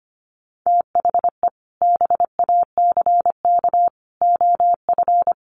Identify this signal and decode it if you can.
Morse code, heard as a keyed tone, about 25 wpm, 700 Hz. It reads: THE BACK OF
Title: Sets of 5 Words Encore2 25wpm